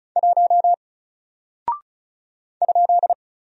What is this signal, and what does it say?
Morse code, heard as a keyed tone, about 36 wpm, 700 Hz. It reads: 1 ?
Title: Single Character (Letter:Number:Punctuation:Prosign) 35wpm